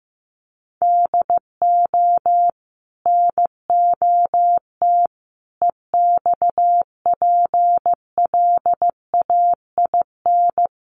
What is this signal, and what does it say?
Morse code, heard as a keyed tone, about 15 wpm, 700 Hz. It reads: DO NOT EXPLAIN
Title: Sentences from Top 1100 Words 15wpm